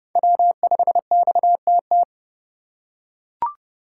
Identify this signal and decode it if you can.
Morse code, heard as a keyed tone, about 32 wpm, 700 Hz. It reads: W5XTT
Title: USA Call Signs - 5 Characters - 3x Character Spacing - 30wpm